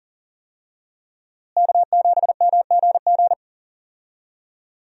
Morse code, heard as a keyed tone, about 40 wpm, 700 Hz. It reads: K7MGG
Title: USA Call Signs - 5 Characters - Repeated 3x - 40wpm